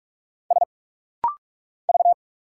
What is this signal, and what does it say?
Morse code, heard as a keyed tone, about 46 wpm, 700 Hz. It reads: S V
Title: Single Letters - Rapid-Fire 45wpm